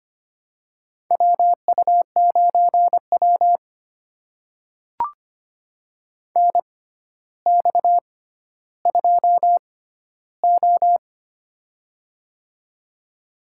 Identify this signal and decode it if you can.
Morse code, heard as a keyed tone, about 25 wpm, spaced as for 13 wpm, 700 Hz. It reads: WU9W NX2O
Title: USA Call Signs - 4 Characters - 6x Character Spacing - 25wpm